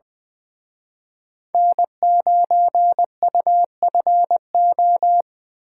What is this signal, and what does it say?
Morse code, heard as a keyed tone, about 24 wpm, 700 Hz. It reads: N9UFO
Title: Speed Racing - USA Call Signs 20wpm